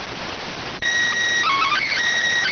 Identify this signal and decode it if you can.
SSTV (slow-scan television), PD 290.